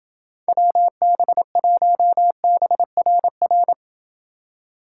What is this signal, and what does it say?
Morse code, heard as a keyed tone, about 29 wpm, 700 Hz. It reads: WB1BRR